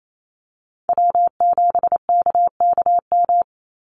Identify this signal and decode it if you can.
Morse code, heard as a keyed tone, about 28 wpm, 700 Hz. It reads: W7KKM
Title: NA SOTA Chaser Callsigns 28wpm